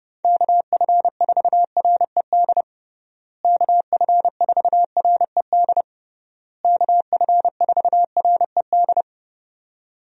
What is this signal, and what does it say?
Morse code, heard as a keyed tone, about 30 wpm, 700 Hz. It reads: KF4RED KF4RED KF4RED
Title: USA Call Signs - 6 Characters - Repeated 3x - 30wpm